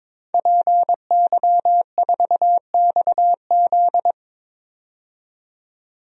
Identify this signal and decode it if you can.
Morse code, heard as a keyed tone, about 22 wpm, 700 Hz. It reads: PY4XZ
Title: DXCC-100 Call Signs - 5 Characters - 22wpm